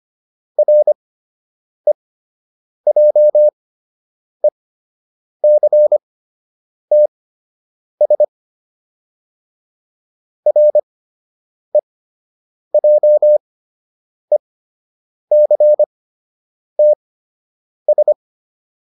Morse code, heard as a keyed tone, about 25 wpm, spaced as for 8 wpm, 600 Hz. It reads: REJECTS REJECTS